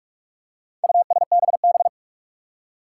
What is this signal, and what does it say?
Morse code, heard as a keyed tone, about 41 wpm, 700 Hz. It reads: USBB
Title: Four Challenging Letter-Number Combinations 45wpm